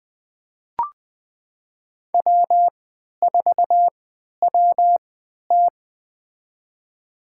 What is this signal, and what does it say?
Morse code, heard as a keyed tone, about 20 wpm, spaced as for 11 wpm, 700 Hz. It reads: W4WT
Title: USA Call Signs - 3x Character Spacing - 20wpm